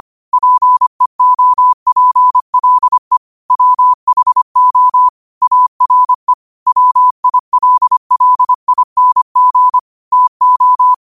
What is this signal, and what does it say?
Morse code, heard as a keyed tone, about 25 wpm, 1000 Hz. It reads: PEOPLE WHO ARE WILLING TO
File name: qotdfiles / 25wpm